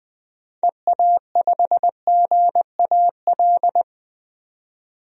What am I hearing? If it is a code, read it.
Morse code, heard as a keyed tone, about 20 wpm, 700 Hz. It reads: EA5GAL
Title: DXCC-100 Call Signs - 6 Characters - 20wpm